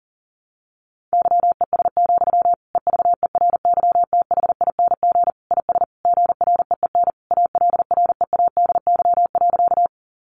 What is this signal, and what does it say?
Morse code, heard as a keyed tone, about 40 wpm, 700 Hz. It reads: YES, EVERYTHING IS GREEN ALREADY.
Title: Sentences from Top 700 Words 40wpm